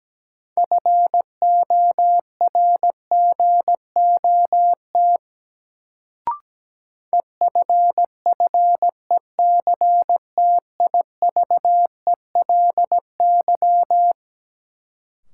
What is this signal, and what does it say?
Morse code, heard as a keyed tone, about 17 wpm, 700 Hz. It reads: FORGOT EFFECTIVELY
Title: Sentences from Top 1400 Words - New Word Drill Alt - 17wpm